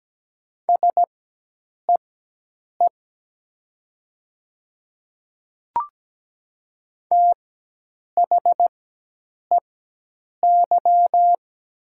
Morse code, heard as a keyed tone, about 17 wpm, spaced as for 8 wpm, 700 Hz. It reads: SEE THEY
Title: 4x Character Spacing - Top 100 Words - 17wpm